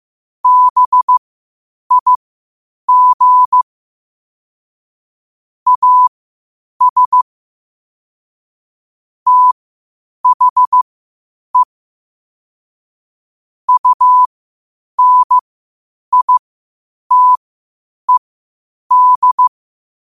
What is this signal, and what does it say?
Morse code, heard as a keyed tone, about 15 wpm, spaced as for 8 wpm, 1000 Hz. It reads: BIG AS THE UNITED
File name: qotdfiles / 8wpm